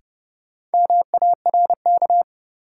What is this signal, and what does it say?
Morse code, heard as a keyed tone, about 32 wpm, 700 Hz. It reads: MARK
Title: Sentences from Top 300 Words - New Word Drill - 30wpm